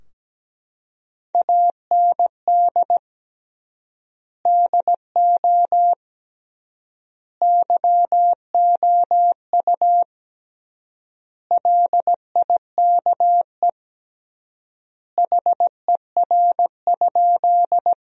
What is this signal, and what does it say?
Morse code, heard as a keyed tone, about 17 wpm, 700 Hz. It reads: AND DO YOU LIKE HER?
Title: 3x Word-Spacing - Sentences from Top 100 Words - 17wpm